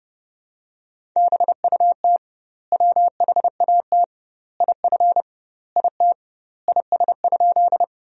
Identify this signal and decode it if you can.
Morse code, heard as a keyed tone, about 30 wpm, 700 Hz. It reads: BUT WHAT IF IT IS?
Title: 2x Word-Spacing - Sentences from Top 100 Words - 30wpm